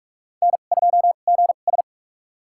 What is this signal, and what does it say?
Morse code, heard as a keyed tone, about 46 wpm, spaced as for 33 wpm, 700 Hz. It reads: N2GS
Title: USA Call Signs - 2x Character Spacing - 45wpm